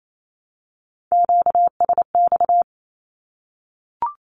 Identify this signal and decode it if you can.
Morse code, heard as a keyed tone, about 28 wpm, 700 Hz. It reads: QSX
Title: Q-Codes Repeated 3x 28wpm